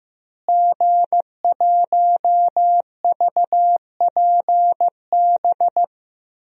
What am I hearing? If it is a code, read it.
Morse code, heard as a keyed tone, about 15 wpm, 700 Hz. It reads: G1VPB
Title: DXCC-100 Call Signs - 5 Characters - Repeated 3x - 15wpm